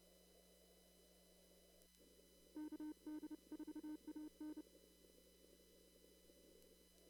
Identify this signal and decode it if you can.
Morse code, heard as a keyed tone, about 30 wpm, 920 Hz. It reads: KD4AN
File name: CW / kt5j wpx cw-op-k5pi